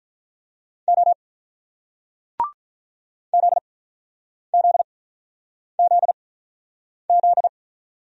Morse code, heard as a keyed tone, about 46 wpm, 700 Hz. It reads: O Z Z Z Z